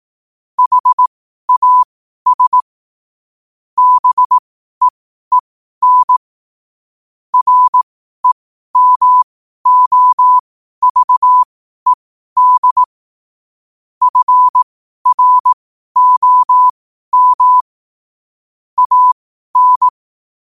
Morse code, heard as a keyed tone, about 18 wpm, spaced as for 12 wpm, 1000 Hz. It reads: HAS BEEN REMOVED FROM AN
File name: qotdfiles / 12wpm